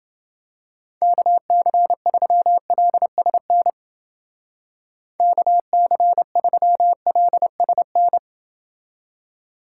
Morse code, heard as a keyed tone, about 32 wpm, 700 Hz. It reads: KC3LSN KC3LSN